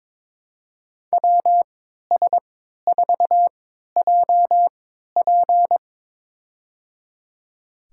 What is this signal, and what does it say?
Morse code, heard as a keyed tone, about 22 wpm, spaced as for 13 wpm, 700 Hz. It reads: WS4JP
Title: USA Call Signs - 5 Characters - 3x Character Spacing - 22wpm